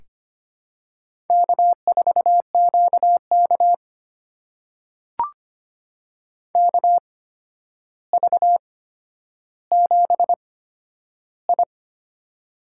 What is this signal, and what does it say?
Morse code, heard as a keyed tone, about 25 wpm, spaced as for 11 wpm, 700 Hz. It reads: K4QK KV7I
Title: USA Call Signs - 4 Characters - 8x Character Spacing - 25wpm